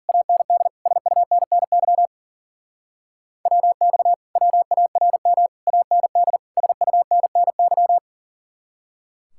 Morse code, heard as a keyed tone, about 50 wpm, 700 Hz. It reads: AND SUNNY WX WARM AND SUNNY